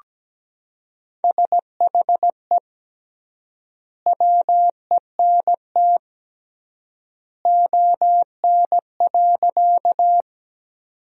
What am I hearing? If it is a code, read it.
Morse code, heard as a keyed tone, about 17 wpm, 700 Hz. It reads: SHE WENT ON.
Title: 3x Word-Spacing - Sentences from Top 200 Words - 17wpm